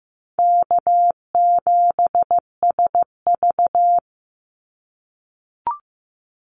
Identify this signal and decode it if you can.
Morse code, heard as a keyed tone, about 15 wpm, 700 Hz. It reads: K7SV
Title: Easy Call Signs - 15wpm